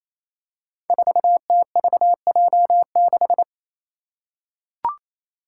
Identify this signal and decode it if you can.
Morse code, heard as a keyed tone, about 28 wpm, 700 Hz. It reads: 4TVJ6
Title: Five Challenging Letter-Number Combinations 28wpm